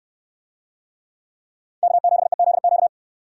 Morse code, heard as a keyed tone, about 65 wpm, 700 Hz. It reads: K7EZQ